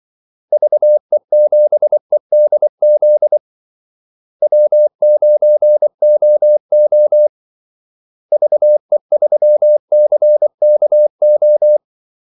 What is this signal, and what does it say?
Morse code, heard as a keyed tone, about 24 wpm, 600 Hz. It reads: VE7EDZ W9OO VE3CKO
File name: Callsigns-24wpm